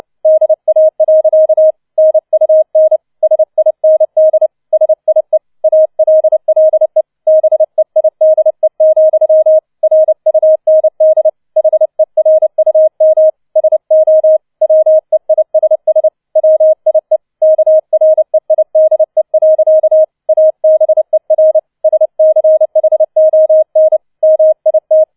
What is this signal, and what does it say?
Morse code, heard as a keyed tone, about 29 wpm, 630 Hz. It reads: DA. NUN SIND SIE ALLE BEIDE, RUND HERUM SO WEISS WIE KREIDE. ABER SCHON MIT
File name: 006 Sechster Streich 30wpm